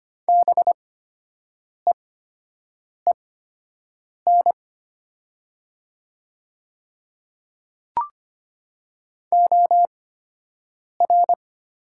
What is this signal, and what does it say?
Morse code, heard as a keyed tone, about 25 wpm, spaced as for 7 wpm, 700 Hz. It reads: BEEN OR
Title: 8x Character Spacing - Top 100 Words - 25wpm